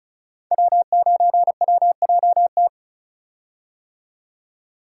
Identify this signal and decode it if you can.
Morse code, heard as a keyed tone, about 36 wpm, 690 Hz. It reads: W9WJT
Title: Code Groups - 5 Letter:Numbers - 35wpm